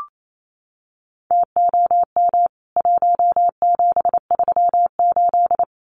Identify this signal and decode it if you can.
Morse code, heard as a keyed tone, about 28 wpm, 700 Hz. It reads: TOM 1738
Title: CWT Exchange 28wpm